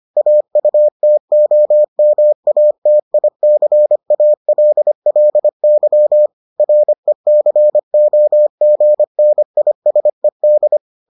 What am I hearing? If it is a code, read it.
Morse code, heard as a keyed tone, about 25 wpm, 600 Hz. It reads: AUTOMATICALLY RECOGNISED